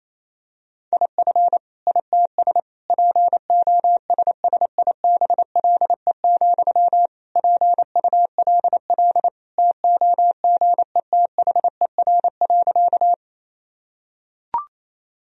Morse code, heard as a keyed tone, about 28 wpm, 700 Hz. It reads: IF ITS POSSIBLE, PULL TOGETHER.
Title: Sentences from Top 1000 Words - Review Part 2 - 28wpm